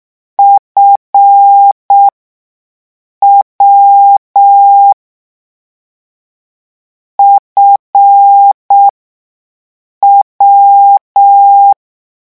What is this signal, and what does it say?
Morse code, heard as a keyed tone, about 6 wpm, spaced as for 5 wpm, 800 Hz. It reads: FW FW
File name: morse mp3 / lecon7-vitesse 6